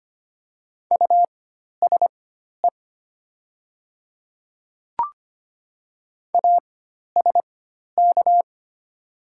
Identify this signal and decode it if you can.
Morse code, heard as a keyed tone, about 25 wpm, spaced as for 12 wpm, 700 Hz. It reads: USE ASK
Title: 4x Character Spacing - Top 200 Words - 25wpm